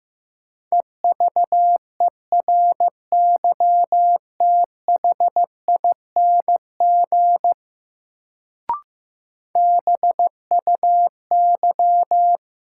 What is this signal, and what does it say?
Morse code, heard as a keyed tone, about 15 wpm, 700 Hz. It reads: EVERYTHING BUY
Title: Sentences from Top 700 Words - New Word Drill - 15wpm